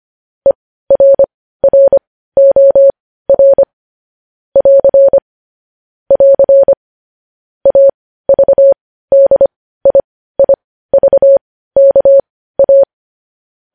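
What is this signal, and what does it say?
Morse code, heard as a keyed tone, about 25 wpm, spaced as for 15 wpm, 550 Hz. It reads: ERROR <AR> <AR> AVDIIVKA
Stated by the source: Morse code news headlines sent at 15 words per minute.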